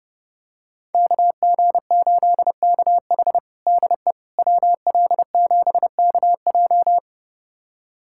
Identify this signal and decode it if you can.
Morse code, heard as a keyed tone, about 32 wpm, 700 Hz. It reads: KG8KH DE WL7KJ
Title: USA Callsign DE Callsign - 5 Characters - 30wpm